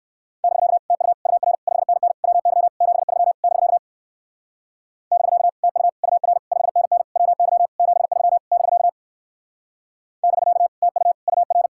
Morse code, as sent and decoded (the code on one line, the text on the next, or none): -...-.- / - ..- / ..- .-. / ..... -. -. / .-- -.-- / --... ...-- / -...-.- / -...-.- / - ..- / ..- .-. / ..... -. -. / .-- -.-- / --... ...-- / -...-.- / -...-.- / - ..- / ..- .-.
<BK> TU UR 5NN WY 73 <BK> <BK> TU UR 5NN WY 73 <BK> <BK> TU UR